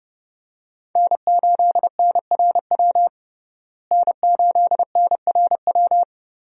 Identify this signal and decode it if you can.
Morse code, heard as a keyed tone, about 30 wpm, 700 Hz. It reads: N8NRW N8NRW
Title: USA Call Signs - 5 Characters - Repeated 2x - 30wpm